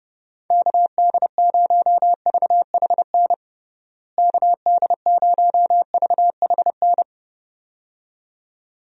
Morse code, heard as a keyed tone, about 30 wpm, 700 Hz. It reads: KD0VHN KD0VHN
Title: USA Call Signs - 6 Characters - Repeated 2x - 30wpm